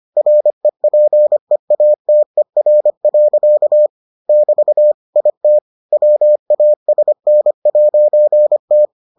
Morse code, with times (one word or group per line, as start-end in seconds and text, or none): 0.2-3.9 REPEATER.
4.3-4.9 <BT>
5.2-5.6 IT
5.9-8.9 WASN'T